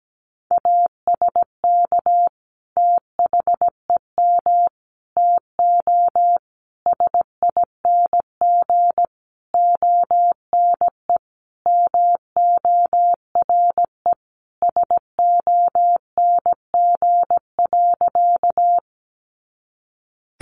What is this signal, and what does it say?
Morse code, heard as a keyed tone, about 17 wpm, 700 Hz. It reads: ASK THEM TO SING ONE MORE SONG.
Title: Sentences from Top 400 Words 17wpm